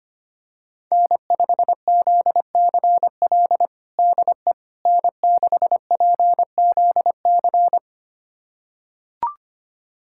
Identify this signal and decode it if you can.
Morse code, heard as a keyed tone, about 25 wpm, 700 Hz. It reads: N5ZCL DE N6PZC
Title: USA Callsign DE Callsign 25wpm